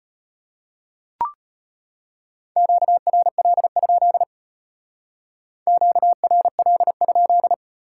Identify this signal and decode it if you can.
Morse code, heard as a keyed tone, about 33 wpm, 700 Hz. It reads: QRL? QRL?